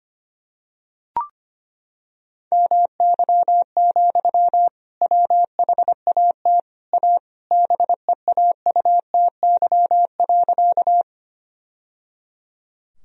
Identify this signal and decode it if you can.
Morse code, heard as a keyed tone, about 25 wpm, 700 Hz. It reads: MY, WHAT A BEAUTY.
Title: Sentences from Top 1000 Words - Review Part 2 - 25wpm